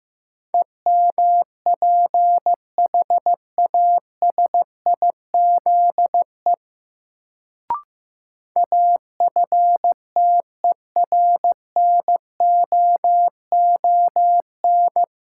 Morse code, heard as a keyed tone, about 15 wpm, 700 Hz. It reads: EMPHASIZE AFTERNOON
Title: Sentences from Top 1500 Words - New Word Drill - 15wpm